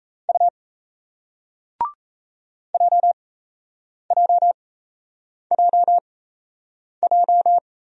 Morse code, heard as a keyed tone, about 35 wpm, spaced as for 8 wpm, 700 Hz. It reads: U JJJJ
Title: Speed-Racing Single Letters 28wpm